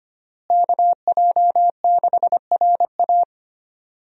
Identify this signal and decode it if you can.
Morse code, heard as a keyed tone, about 25 wpm, 700 Hz. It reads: KJ6RA
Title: USA Call Signs - 5 Characters - Repeated 3x - 25wpm